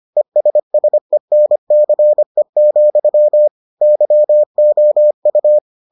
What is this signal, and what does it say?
Morse code, heard as a keyed tone, about 25 wpm, 600 Hz. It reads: ESSENCE, YOU